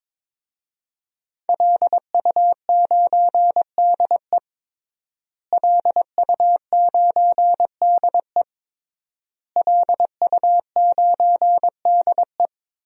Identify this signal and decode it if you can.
Morse code, heard as a keyed tone, about 22 wpm, 700 Hz. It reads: LU9DE LU9DE LU9DE
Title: DXCC-100 Call Signs - 5 Characters - Repeated 3x - 22wpm